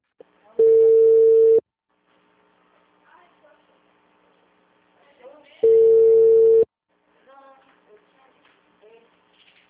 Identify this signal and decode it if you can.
Ringback tone (China, 450 Hz), with speech.